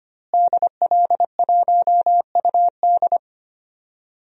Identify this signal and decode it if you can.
Morse code, heard as a keyed tone, about 25 wpm, 700 Hz. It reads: DL1UD
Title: DXCC-100 Call Signs - 5 Characters - Repeated 3x - 25wpm